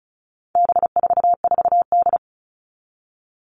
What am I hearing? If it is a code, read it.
Morse code, heard as a keyed tone, about 34 wpm, 700 Hz. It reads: B44D
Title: All Cut Numbers - 4-Digit 35wpm